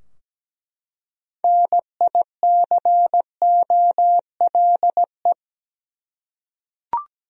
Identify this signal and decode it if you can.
Morse code, heard as a keyed tone, about 17 wpm, 700 Hz. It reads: NICOLE
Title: USA Common Names - 17wpm